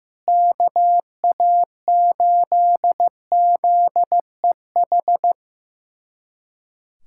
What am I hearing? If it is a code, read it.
Morse code, heard as a keyed tone, about 15 wpm, 700 Hz. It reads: KA8ZEH
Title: USA Callsign DE Callsign - 6 Characters - 15wpm